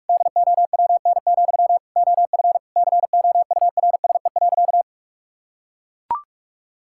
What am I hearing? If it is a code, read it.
Morse code, heard as a keyed tone, about 42 wpm, 700 Hz. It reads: DOWN, OF COURSE.
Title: Sentences from Top 1000 Words - Review Part 5 - 45wpm